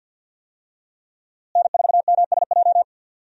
Morse code, heard as a keyed tone, about 51 wpm, 700 Hz. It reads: N4MSJ
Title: USA Call Signs - 5 Characters - Repeated 2x - 50wpm